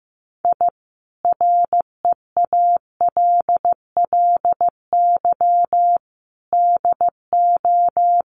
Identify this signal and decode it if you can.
Morse code, heard as a keyed tone, about 15 wpm, 700 Hz. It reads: I REALLY DO
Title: Sets of 5 Words Encore2 15wpm